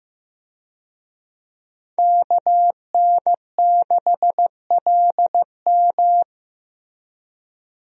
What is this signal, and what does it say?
Morse code, heard as a keyed tone, about 15 wpm, 700 Hz. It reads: KN6LM
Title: USA Call Signs - 5 Characters - Repeated 3x - 15wpm